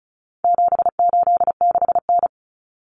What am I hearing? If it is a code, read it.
Morse code, heard as a keyed tone, about 34 wpm, 700 Hz. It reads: 786N
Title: Common Cut Numbers - 4-Digit 35wpm